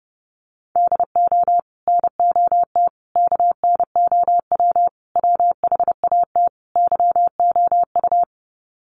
Morse code, heard as a keyed tone, about 30 wpm, 700 Hz. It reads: DO NOT KNOW WHAT YOU
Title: Sets of 5 Words 30wpm